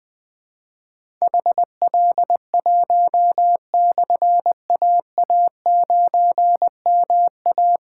Morse code, heard as a keyed tone, about 20 wpm, 700 Hz. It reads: HL1/AA9MA
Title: Difficult Call Signs 20wpm